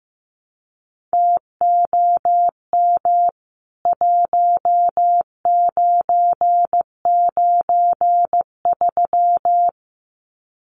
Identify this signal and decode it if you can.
Morse code, heard as a keyed tone, about 15 wpm, 700 Hz. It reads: TOM 1993
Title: CWT Exchange 15wpm